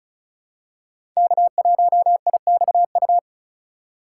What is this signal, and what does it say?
Morse code, heard as a keyed tone, about 34 wpm, 700 Hz. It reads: K1IXU
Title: USA Call Signs - 5 Characters - 3x Character Spacing - 35wpm